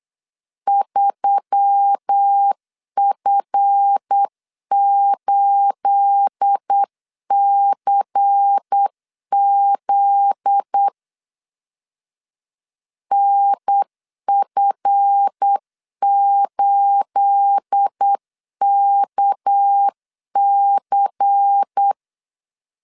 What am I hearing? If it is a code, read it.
Morse code, heard as a keyed tone, about 8 wpm, 800 Hz. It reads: 3F8CZ NF8KC
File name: LEZIONI / 9 28 8wpm 3512